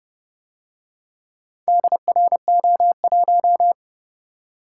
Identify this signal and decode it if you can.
Morse code, heard as a keyed tone, about 29 wpm, 700 Hz. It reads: DRO1